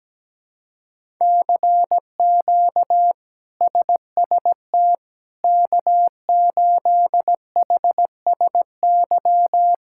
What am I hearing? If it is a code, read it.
Morse code, heard as a keyed tone, about 17 wpm, 700 Hz. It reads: CQ SST K8HSY
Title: K1USN SST QSO Practice - 17wpm